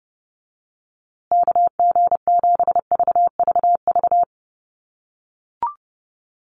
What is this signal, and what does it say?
Morse code, heard as a keyed tone, about 28 wpm, 700 Hz. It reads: KG7VVV
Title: USA Call Signs Repeated 3x 30wpm